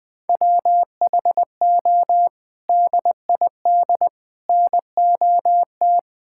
Morse code, heard as a keyed tone, about 20 wpm, 700 Hz. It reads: WHO DID NOT
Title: Sets of 6 Words - 20wpm